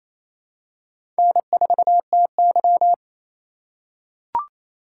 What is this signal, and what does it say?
Morse code, heard as a keyed tone, about 28 wpm, 700 Hz. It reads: N4TY
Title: USA Call Signs - 4 Characters - 4x Character Spacing - 28wpm